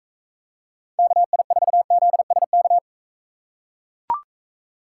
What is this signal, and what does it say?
Morse code, heard as a keyed tone, about 42 wpm, 700 Hz. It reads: KI4ZSK